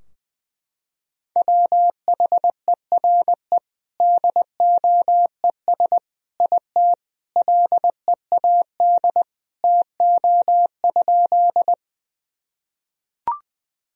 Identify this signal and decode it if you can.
Morse code, heard as a keyed tone, about 21 wpm, 700 Hz. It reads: WHERE DOES IT LEAD TO?
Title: Sentences from Top 1000 Words - Review Part 2 - 20wpm